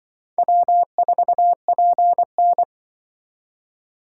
Morse code, heard as a keyed tone, about 25 wpm, 700 Hz. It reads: W4PN
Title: Speed Racing - USA 1x2 Call Signs 20wpm